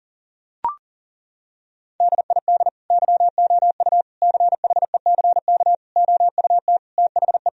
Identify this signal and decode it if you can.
Morse code, heard as a keyed tone, about 37 wpm, 700 Hz. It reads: DID YOU CHECK OUT THE
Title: Sentences from Top 1000 Words - Review Part 4 - 40wpm